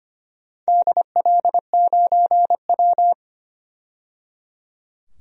Morse code, heard as a keyed tone, about 25 wpm, 700 Hz. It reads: DL9W
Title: DXCC-100 Call Signs - 4 Characters - 25wpm